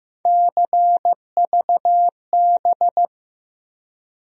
Morse code, heard as a keyed tone, about 15 wpm, 700 Hz. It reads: CVB
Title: USA Call Sign Postfixes - 3 Letters 15wpm